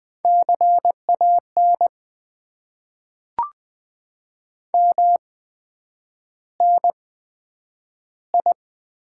Morse code, heard as a keyed tone, about 20 wpm, spaced as for 9 wpm, 700 Hz. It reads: CAN MNI
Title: Top 90 QSO Elements - 8x Character Spacing - 20wpm